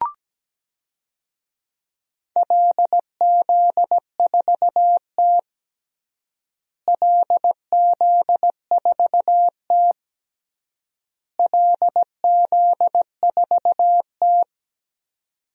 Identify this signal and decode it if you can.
Morse code, heard as a keyed tone, about 17 wpm, 700 Hz. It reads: LZ4T LZ4T LZ4T
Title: DXCC-100 Call Signs - 4 Characters - Repeated 3x - 17wpm